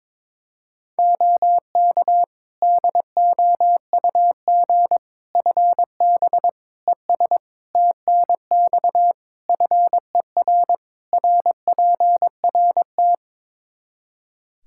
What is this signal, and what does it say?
Morse code, heard as a keyed tone, about 22 wpm, 700 Hz. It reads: OK DOUG FB ES TNX FER RPRT
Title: Practice QSO - Exchange 2 - Info - 22wpm